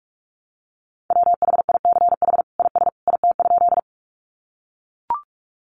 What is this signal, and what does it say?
Morse code, heard as a keyed tone, about 44 wpm, 700 Hz. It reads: WHICH IS IT?
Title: Sentences from Top 100 Words 45wpm